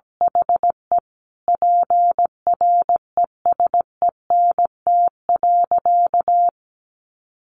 Morse code, heard as a keyed tone, about 17 wpm, 700 Hz. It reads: HE PRESENT.
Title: Sentences from Top 900 Words 17wpm